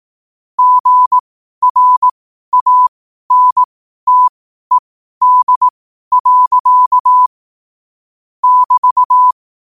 Morse code, heard as a keyed tone, about 18 wpm, spaced as for 12 wpm, 1000 Hz. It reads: GRANTED. <BT>